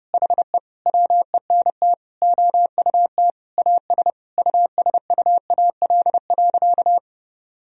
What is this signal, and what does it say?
Morse code, heard as a keyed tone, about 30 wpm, 700 Hz. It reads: HE WENT OUT AS USUAL.
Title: Sentences from Top 1000 Words - Review Part 2 - 30wpm